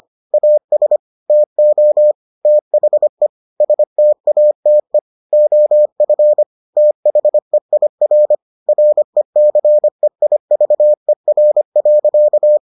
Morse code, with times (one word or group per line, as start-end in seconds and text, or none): 0.3-1.0 AS
1.3-2.1 TO
2.5-3.3 THE
3.6-5.0 STATE
5.3-6.4 OF
6.8-8.3 THEIR
8.7-12.6 RECEIVER.